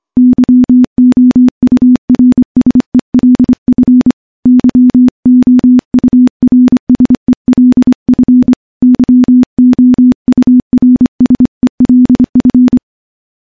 Morse code, heard as a keyed tone, about 25 wpm, 250 Hz. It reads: YOURSELF YOURSELF YOURSELF
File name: morse-training-25WPM